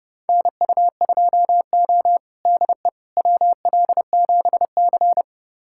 Morse code, heard as a keyed tone, about 30 wpm, 700 Hz. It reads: NU2O DE WL7C
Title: USA Callsign DE Callsign - 4 Characters - 30wpm